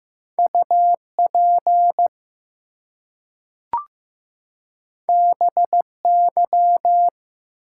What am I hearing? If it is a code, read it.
Morse code, heard as a keyed tone, about 15 wpm, 700 Hz. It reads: UP BY
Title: Two Letter Words - 15wpm